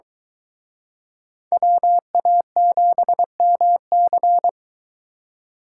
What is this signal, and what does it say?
Morse code, heard as a keyed tone, about 23 wpm, 700 Hz. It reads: WA7MC